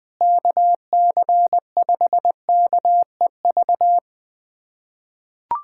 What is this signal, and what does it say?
Morse code, heard as a keyed tone, about 20 wpm, 700 Hz. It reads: KC5KEV
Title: USA Callsign DE Callsign - 6 Characters - 20wpm